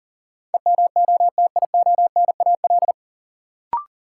Morse code, heard as a keyed tone, about 40 wpm, 700 Hz. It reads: EMOTIONAL
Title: Sentences from Top 1000 Words - Review Part 5 - New Word Drill - 40wpm